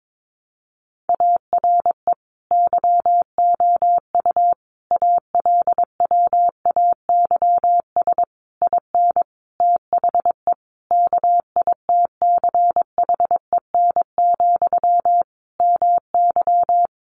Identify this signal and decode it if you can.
Morse code, heard as a keyed tone, about 22 wpm, 700 Hz. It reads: ARE YOU ALWAYS IN THE KITCHEN, MY
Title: Sentences from Top 1400 Words 22wpm